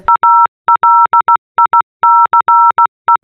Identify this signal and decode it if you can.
Morse code, heard as a keyed tone, about 16 wpm, 940 Hz. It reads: ALICE
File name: telegrafo-5a-disney